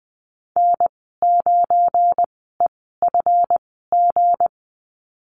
Morse code, heard as a keyed tone, about 20 wpm, spaced as for 15 wpm, 700 Hz. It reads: N9EFG
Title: 193 Lesson 31g - Letter v - Call Signs 20wpm